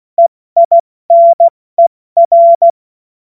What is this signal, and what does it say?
Morse code, heard as a keyed tone, about 16 wpm, spaced as for 14 wpm, 680 Hz. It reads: EINER
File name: Morsekurs / Witze / Witz 16. August 70 F80